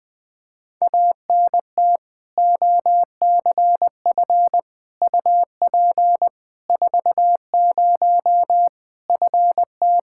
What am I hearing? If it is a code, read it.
Morse code, heard as a keyed tone, about 20 wpm, 700 Hz. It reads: ANT OCF UP 40 FT
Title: K1USN SST QSO Practice 20wpm